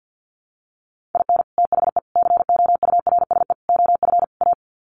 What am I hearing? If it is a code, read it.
Morse code, heard as a keyed tone, about 50 wpm, 700 Hz. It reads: IN THE COURSE OF A
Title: Sets of 5 Words Encore2 50wpm